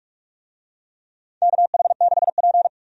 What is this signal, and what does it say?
Morse code, heard as a keyed tone, about 42 wpm, 700 Hz. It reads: KH6P
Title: DXCC-100 Call Signs - 4 Characters - 45wpm